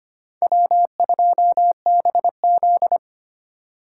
Morse code, heard as a keyed tone, about 25 wpm, 700 Hz. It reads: W2BZ
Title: USA Call Signs - 4 Characters - 4x Character Spacing - 25wpm